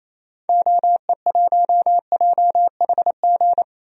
Morse code, heard as a keyed tone, about 27 wpm, 700 Hz. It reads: OE1JHG
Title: DXCC-100 Call Signs - 6 Characters - Repeated 3x - 28wpm